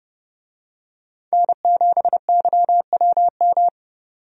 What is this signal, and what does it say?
Morse code, heard as a keyed tone, about 29 wpm, 700 Hz. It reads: N7YWM
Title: USA Call Signs - 5 Characters - 30wpm